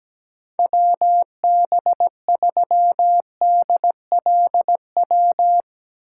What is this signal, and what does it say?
Morse code, heard as a keyed tone, about 17 wpm, 700 Hz. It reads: WB3DLW
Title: USA Call Signs - 6 Characters - 3x Character Spacing - 17wpm